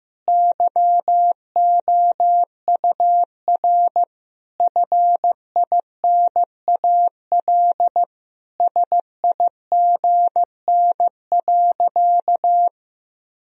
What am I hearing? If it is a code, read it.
Morse code, heard as a keyed tone, about 15 wpm, 700 Hz. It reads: YOUR FINAL SIGN.
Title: Sentences from Top 1000 Words - Review Part 2 - 15wpm